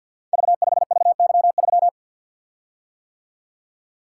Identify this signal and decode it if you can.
Morse code, heard as a keyed tone, about 51 wpm, 700 Hz. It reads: V5VY3
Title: Code Groups - 5 Letter:Numbers - 50wpm